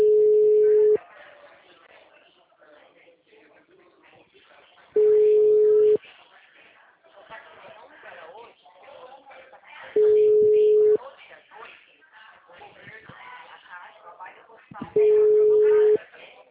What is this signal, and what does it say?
Ringback tone (Europe, 425 Hz), with speech.